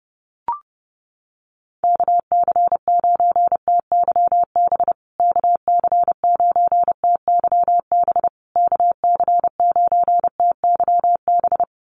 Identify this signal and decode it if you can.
Morse code, heard as a keyed tone, about 30 wpm, 700 Hz. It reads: KC9TYB KC9TYB KC9TYB
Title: USA Call Signs Repeated 3x 30wpm